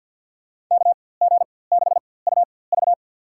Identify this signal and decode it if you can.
Morse code, heard as a keyed tone, about 53 wpm, spaced as for 24 wpm, 700 Hz. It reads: KG6UV
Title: USA Call Signs - 5 Characters - 4x Character Spacing - 50wpm